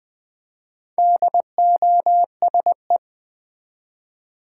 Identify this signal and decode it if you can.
Morse code, heard as a keyed tone, about 20 wpm, 700 Hz. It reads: DOSE
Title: Build A Word - 1 Letter At A Time - 4-Letter Words - 20wpm